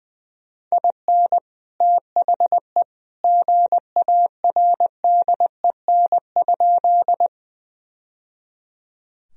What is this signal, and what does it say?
Morse code, heard as a keyed tone, about 20 wpm, 700 Hz. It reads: IN THE GARDEN?
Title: Sentences from Top 1000 Words - Review Part 4 - 20wpm